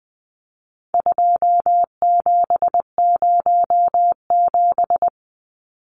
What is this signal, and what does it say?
Morse code, heard as a keyed tone, about 20 wpm, 700 Hz. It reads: 2707
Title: Numbers - 4-Digit 20wpm